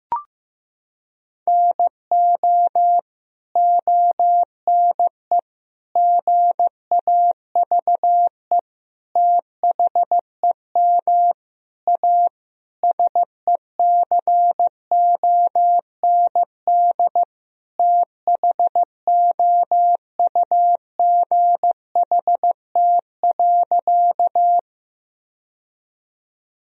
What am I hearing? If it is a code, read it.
Morse code, heard as a keyed tone, about 15 wpm, 700 Hz. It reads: NO ONE GAVE THEM A SECOND THOUGHT.
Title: Sentences from Top 1000 Words - Review Part 1 - 15wpm